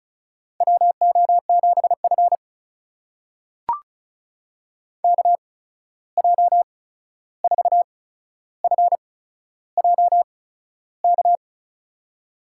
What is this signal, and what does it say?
Morse code, heard as a keyed tone, about 36 wpm, spaced as for 13 wpm, 700 Hz. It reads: WO7F KJ4FJK
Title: USA Call Signs - 8x Character Spacing - 35wpm